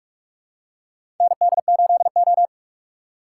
Morse code, heard as a keyed tone, about 45 wpm, 700 Hz. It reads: ND8O
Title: USA Call Signs - 4 Characters - 4x Character Spacing - 45wpm